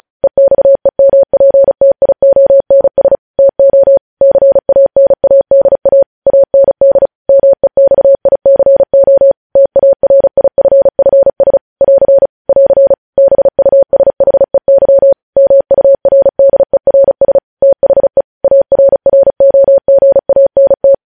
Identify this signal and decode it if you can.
Morse code, heard as a keyed tone, about 36 wpm, 550 Hz. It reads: EXEMPTIONS TO CANADA AND MEXICO TARIFFS <AR> <AR> BUSHEY MURDERS THE ARROGANT
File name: podcast / News headlines-1x-35x35